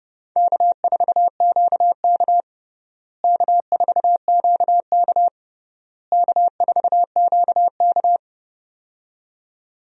Morse code, heard as a keyed tone, about 32 wpm, 700 Hz. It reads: K4QK K4QK K4QK
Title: USA Call Signs - 4 Characters - Repeated 3x - 30wpm